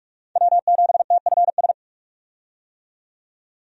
Morse code, heard as a keyed tone, about 44 wpm, 700 Hz. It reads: W7TFS
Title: USA Call Signs - 5 Characters - 45wpm